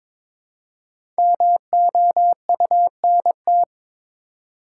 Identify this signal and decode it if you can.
Morse code, heard as a keyed tone, about 22 wpm, 690 Hz. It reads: MOUNT
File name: 5 Letter Words - First 3 Letters Given - 22wpm